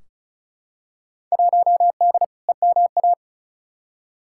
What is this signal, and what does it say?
Morse code, heard as a keyed tone, about 34 wpm, 700 Hz. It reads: 1D EMA
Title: Field Day Exchanges - 35wpm